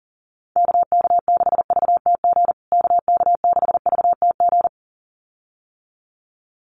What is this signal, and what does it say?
Morse code, heard as a keyed tone, about 40 wpm, 700 Hz. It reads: KK6VTG KK6VTG
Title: USA Call Signs Repeated 2x 40wpm